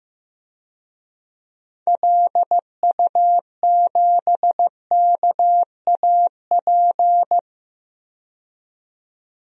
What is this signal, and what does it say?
Morse code, heard as a keyed tone, about 15 wpm, 700 Hz. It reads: LU7KAP
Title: DXCC-100 Call Signs - 6 Characters - 15wpm